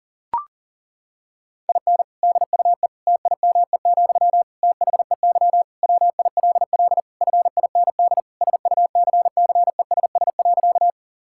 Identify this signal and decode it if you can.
Morse code, heard as a keyed tone, about 37 wpm, 700 Hz. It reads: IN DUE TIME, THEY WILL FIND SUCCESS.
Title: Sentences from Top 1000 Words - Review Part 6 - 40wpm